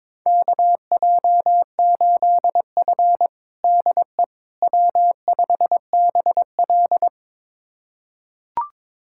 Morse code, heard as a keyed tone, about 21 wpm, 700 Hz. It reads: KJ8F DE W5BL
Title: USA Callsign DE Callsign - 4 Characters - 22wpm